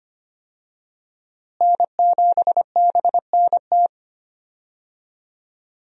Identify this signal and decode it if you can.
Morse code, heard as a keyed tone, about 25 wpm, 700 Hz. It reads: N7BNT
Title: USA Call Signs - 5 Characters - Repeated 2x - 25wpm